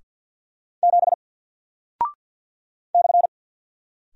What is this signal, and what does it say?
Morse code, heard as a keyed tone, about 48 wpm, 700 Hz. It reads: 7 /
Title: Single Character (Letter:Number:Punctuation:Prosign) Alt 50wpm